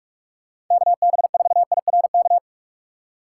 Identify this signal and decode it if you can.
Morse code, heard as a keyed tone, about 41 wpm, 700 Hz. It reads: KB4IRK
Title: USA Call Signs - 6 Characters - 3x Character Spacing - 45wpm